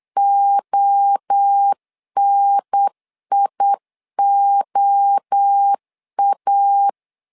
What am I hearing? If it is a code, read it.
Morse code, heard as a keyed tone, about 8 wpm, 800 Hz. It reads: ONIOA
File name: 4 Q 8wpm